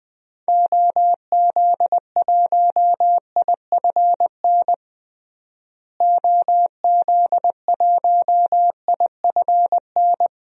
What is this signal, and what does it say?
Morse code, heard as a keyed tone, about 20 wpm, 700 Hz. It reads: OZ1IFN OZ1IFN
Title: DXCC-100 Call Signs - 6 Characters - Repeated 3x - 20wpm